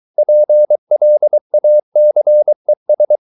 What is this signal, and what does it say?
Morse code, heard as a keyed tone, about 23 wpm, 600 Hz. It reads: PLACES